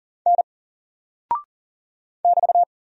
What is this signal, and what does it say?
Morse code, heard as a keyed tone, about 38 wpm, 700 Hz. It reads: N <BT>
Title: Single Character (Letter:Number:Punctuation:Prosign) Alt 40wpm